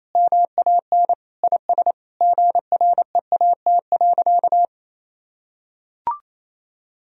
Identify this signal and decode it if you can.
Morse code, heard as a keyed tone, about 28 wpm, 700 Hz. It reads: MAN IS GREAT.
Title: Sentences from Top 1000 Words - Review Part 4 - 28wpm